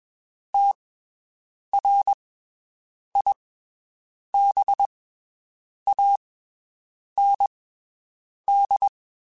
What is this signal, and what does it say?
Morse code, heard as a keyed tone, about 22 wpm, spaced as for 7 wpm, 800 Hz. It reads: TRIBAND
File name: (spaced) / QSOs Spaced-mp3s / WD0BGP 1 @20 sp